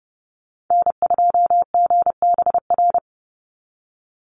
Morse code, heard as a keyed tone, about 30 wpm, 700 Hz. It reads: N2GBR
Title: NA SOTA Chaser Callsigns 30wpm